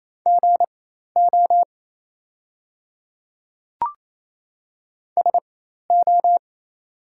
Morse code, heard as a keyed tone, about 27 wpm, spaced as for 13 wpm, 690 Hz. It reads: GO SO
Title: 4x Character Spacing - Top 100 Words - 28wpm